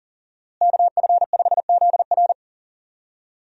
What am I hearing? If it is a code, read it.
Morse code, heard as a keyed tone, about 40 wpm, 700 Hz. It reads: KF5ZR
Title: USA Call Signs - 5 Characters - Repeated 2x - 40wpm